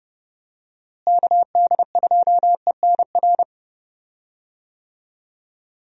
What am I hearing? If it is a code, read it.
Morse code, heard as a keyed tone, about 31 wpm, 700 Hz. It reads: KD2ENR
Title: USA Call Signs - 6 Characters - Repeated 2x - 30wpm